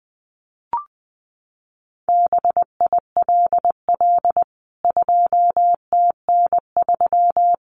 Morse code, heard as a keyed tone, about 19 wpm, 700 Hz. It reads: BILL 2TN3
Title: CWT Exchange 20wpm